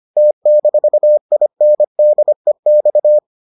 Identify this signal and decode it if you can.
Morse code, heard as a keyed tone, about 25 wpm, 600 Hz. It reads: T-INDEX